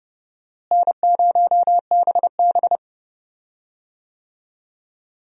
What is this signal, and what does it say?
Morse code, heard as a keyed tone, about 32 wpm, 700 Hz. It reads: N0BB
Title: USA Call Signs - 4 Characters - Repeated 3x - 30wpm